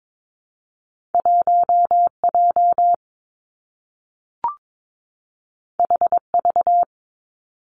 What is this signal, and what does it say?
Morse code, heard as a keyed tone, about 22 wpm, 700 Hz. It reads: 1 J H V
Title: Two Challenging Letter-Number Combinations 22wpm